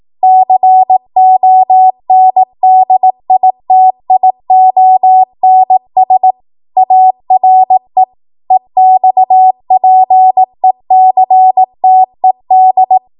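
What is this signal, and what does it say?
Morse code, heard as a keyed tone, about 18 wpm, 750 Hz. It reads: CONDITIONS ARE EXPECTED
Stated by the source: Code practice at 18 WPM, a transition run bridging 15 to 20 WPM.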